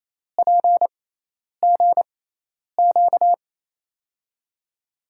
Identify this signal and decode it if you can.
Morse code, heard as a keyed tone, about 28 wpm, spaced as for 10 wpm, 700 Hz. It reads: PGQ
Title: USA Call Signs - 6x Character Spacing - 28wpm